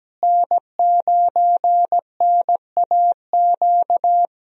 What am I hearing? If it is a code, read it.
Morse code, heard as a keyed tone, about 17 wpm, 700 Hz. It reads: N9NAQ
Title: USA Call Signs - 5 Characters - Repeated 3x - 17wpm